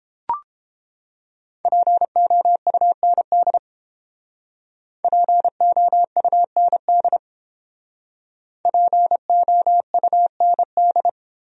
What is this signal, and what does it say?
Morse code, heard as a keyed tone, about 31 wpm, 700 Hz. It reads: POUND POUND POUND